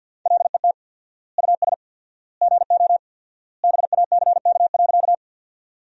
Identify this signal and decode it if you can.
Morse code, heard as a keyed tone, about 53 wpm, 700 Hz. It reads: LET US GO BACK.
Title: 4x Word-Spacing - Sentences from Top 300 Words - 50wpm